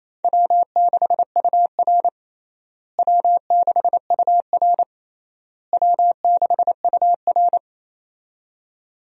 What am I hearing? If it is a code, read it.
Morse code, heard as a keyed tone, about 28 wpm, 700 Hz. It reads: W6UR W6UR W6UR
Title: USA Call Signs - 4 Characters - Repeated 3x - 28wpm